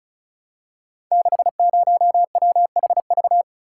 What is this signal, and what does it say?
Morse code, heard as a keyed tone, about 36 wpm, 690 Hz. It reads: 60WHV
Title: Code Groups - 5 Letter:Numbers - 35wpm